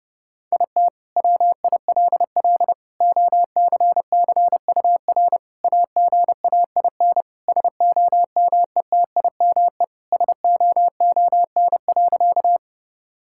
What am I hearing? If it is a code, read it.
Morse code, heard as a keyed tone, about 30 wpm, 700 Hz. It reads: IT WILL OCCUR AGAIN SOMETIME SOON.
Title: Sentences from Top 1000 Words - Review Part 6 - 30wpm